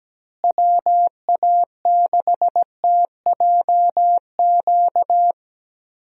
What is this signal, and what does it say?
Morse code, heard as a keyed tone, about 17 wpm, 700 Hz. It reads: WA6TJQ
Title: USA Call Signs - 6 Characters - Repeated 3x - 17wpm